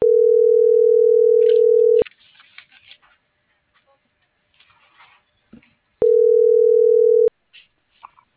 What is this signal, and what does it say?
Ringback tone (North America), 440 + 480 Hz.